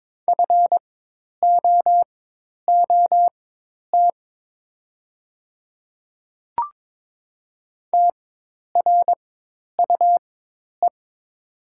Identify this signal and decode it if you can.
Morse code, heard as a keyed tone, about 22 wpm, spaced as for 10 wpm, 690 Hz. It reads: FOOT TRUE
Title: 4x Character Spacing - Top 500 Words - 22wpm